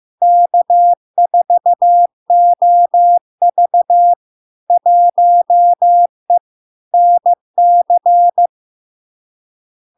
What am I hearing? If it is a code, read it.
Morse code, heard as a keyed tone, about 15 wpm, 700 Hz. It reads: K4OV 1E NC
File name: LARK-FD-2022-Log-in-CW-15wpm-700-Hz